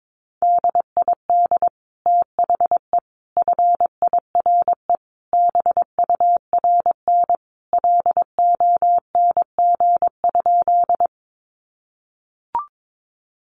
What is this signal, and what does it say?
Morse code, heard as a keyed tone, about 22 wpm, 700 Hz. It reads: DID THE FIRE BURN LONG?
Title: Sentences from Top 1300 Words 22wpm